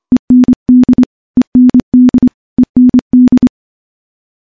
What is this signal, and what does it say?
Morse code, heard as a keyed tone, about 25 wpm, 250 Hz. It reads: END END END
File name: morse-training-25WPM